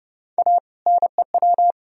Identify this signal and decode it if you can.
Morse code, heard as a keyed tone, about 30 wpm, 700 Hz. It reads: A NEW
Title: Sentences from Top 1000 Words - Review Part 2 - 30wpm